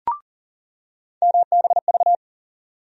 Morse code, heard as a keyed tone, about 40 wpm, 700 Hz. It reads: MBV
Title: USA Call Sign Postfixes - 3 Letters 40wpm